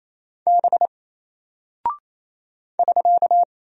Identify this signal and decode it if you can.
Morse code, heard as a keyed tone, about 27 wpm, 700 Hz. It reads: B <SK>
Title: Single Character (Letter:Number:Punctuation:Prosign) Alt 28wpm